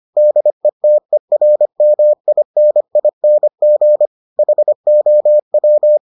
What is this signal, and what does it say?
Morse code, heard as a keyed tone, about 25 wpm, 600 Hz. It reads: DETERMINING HOW